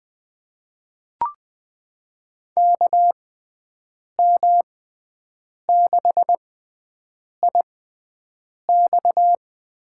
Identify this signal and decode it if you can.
Morse code, heard as a keyed tone, about 20 wpm, spaced as for 7 wpm, 700 Hz. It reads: KM6IX
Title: USA Call Signs - 6x Character Spacing - 20wpm